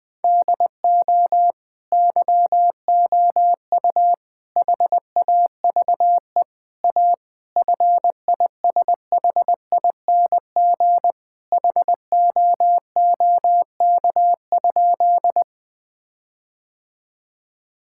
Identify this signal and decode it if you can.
Morse code, heard as a keyed tone, about 21 wpm, 700 Hz. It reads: DO YOU HAVE A FISHING HOOK?
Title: Sentences from Top 1600 Words 20wpm